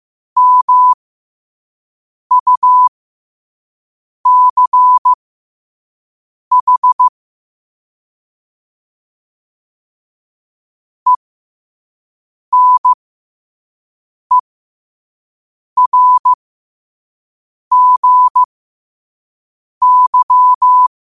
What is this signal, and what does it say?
Morse code, heard as a keyed tone, about 15 wpm, spaced as for 5 wpm, 1000 Hz. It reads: MUCH ENERGY
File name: qotdfiles / 5wpm